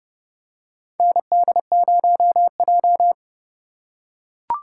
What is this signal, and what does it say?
Morse code, heard as a keyed tone, about 31 wpm, 700 Hz. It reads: ND0J
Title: USA Call Signs - 4 Characters - 2x Character Spacing - 30wpm